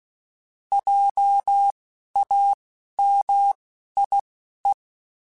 Morse code, heard as a keyed tone, about 16 wpm, spaced as for 11 wpm, 800 Hz. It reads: JAMIE
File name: Alphabetical / Alphabetical mp3s / Js @10